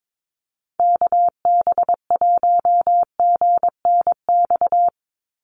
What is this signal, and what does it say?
Morse code, heard as a keyed tone, about 22 wpm, 700 Hz. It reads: KB1GNX
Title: USA Call Signs 22wpm